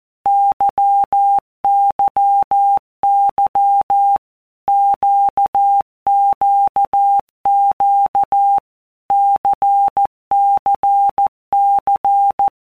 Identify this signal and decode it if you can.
Morse code, heard as a keyed tone, about 14 wpm, 800 Hz. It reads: YYY QQQ CCC
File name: Basic Letter Groups By WPM's / Groups@13 / LFCYQ@13